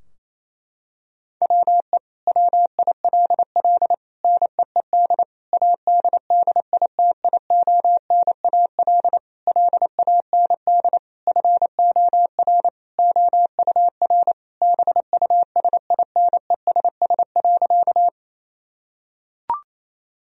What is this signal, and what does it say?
Morse code, heard as a keyed tone, about 28 wpm, 700 Hz. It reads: WE WILL NEED ADDITIONAL LAND FOR OUR BUSINESS.
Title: Sentences from Top 1000 Words - Review Part 4 - 28wpm